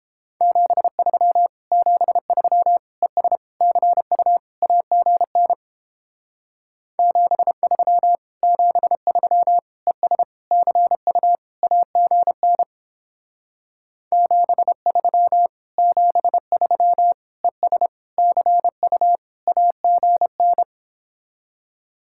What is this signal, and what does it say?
Morse code, heard as a keyed tone, about 31 wpm, 700 Hz. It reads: 73 73 ES CU AGN 73 73 ES CU AGN 73 73 ES CU AGN